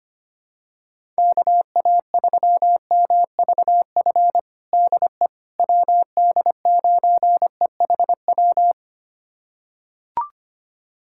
Morse code, heard as a keyed tone, about 25 wpm, 700 Hz. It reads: KA3MVF DE WD9EHW
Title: USA Callsign DE Callsign 25wpm